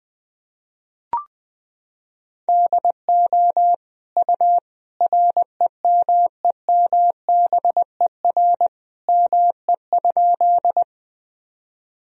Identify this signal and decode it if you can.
Morse code, heard as a keyed tone, about 21 wpm, 700 Hz. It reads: DO U REMEMBER ME?
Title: Sentences from Top 400 Words and Abbreviations 20wpm